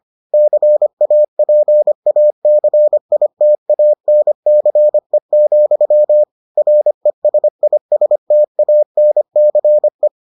Morse code, heard as a keyed tone, about 25 wpm, 600 Hz. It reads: CAPACITANCE, RESISTANCE